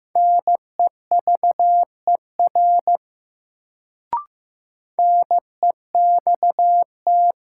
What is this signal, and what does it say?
Morse code, heard as a keyed tone, about 16 wpm, 700 Hz. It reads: NEVER NEXT
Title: Sentences from Top 300 Words - New Word Drill - 15wpm